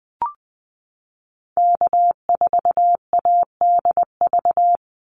Morse code, heard as a keyed tone, about 20 wpm, 700 Hz. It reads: K4ADV
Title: NA SOTA Activator Callsigns 20wpm